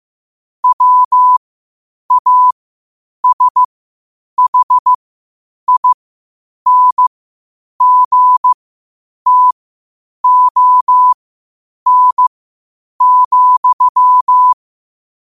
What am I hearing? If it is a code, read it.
Morse code, heard as a keyed tone, about 15 wpm, spaced as for 8 wpm, 1000 Hz. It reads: WASHINGTON,
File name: qotdfiles / 8wpm